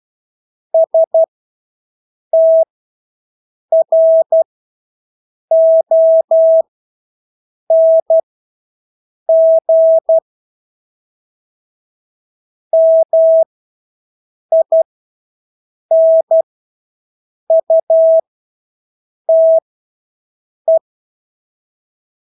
Morse code, heard as a keyed tone, about 12 wpm, spaced as for 6 wpm, 650 Hz. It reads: STRONG MINUTE
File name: Engelske ord 12-6 wpm